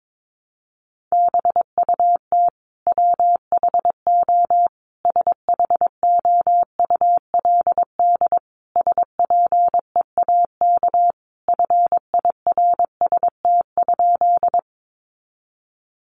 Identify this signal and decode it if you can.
Morse code, heard as a keyed tone, about 22 wpm, 700 Hz. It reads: BUT WHO SHOULD SPEAK FIRST?
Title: Sentences from Top 1100 Words 22wpm